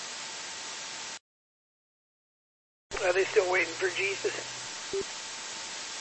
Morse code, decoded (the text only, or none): none